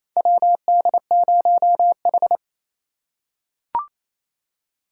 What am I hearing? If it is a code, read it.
Morse code, heard as a keyed tone, about 28 wpm, 700 Hz. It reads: WD0H
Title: USA Call Signs - 4 Characters - 3x Character Spacing - 28wpm